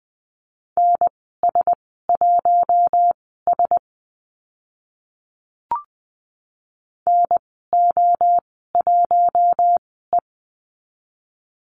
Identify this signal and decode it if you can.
Morse code, heard as a keyed tone, about 20 wpm, spaced as for 14 wpm, 700 Hz. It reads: NS1S NO1E
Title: 042 Lesson 8g - Number 1 - Call Signs 20wpm